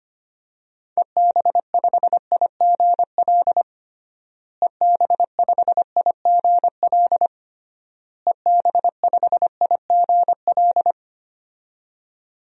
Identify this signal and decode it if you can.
Morse code, heard as a keyed tone, about 25 wpm, 700 Hz. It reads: EB5IGL EB5IGL EB5IGL
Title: DXCC-100 Call Signs - 6 Characters - Repeated 3x - 25wpm